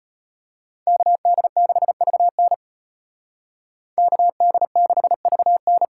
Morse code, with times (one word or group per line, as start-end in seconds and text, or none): 0.9-2.5 KD6VN
4.0-5.8 KD6VN